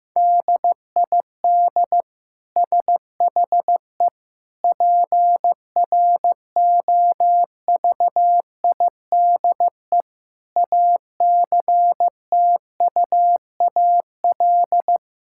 Morse code, heard as a keyed tone, about 15 wpm, 700 Hz. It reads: DID SHE PROVIDE ACTUAL
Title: Sentences from Top 1000 Words - Review Part 5 - 15wpm